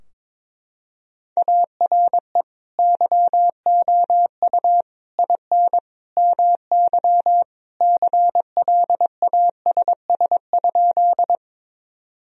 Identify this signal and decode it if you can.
Morse code, heard as a keyed tone, about 21 wpm, 700 Hz. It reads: ARE YOU IN MY CLASS?
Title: Sentences from Top 1000 Words - Review Part 2 - 22wpm